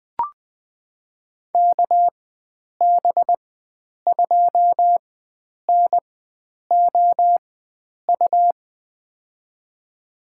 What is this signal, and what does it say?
Morse code, heard as a keyed tone, about 21 wpm, spaced as for 9 wpm, 700 Hz. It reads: KB2NOU
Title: USA Call Signs - 6 Characters - 4x Character Spacing - 20wpm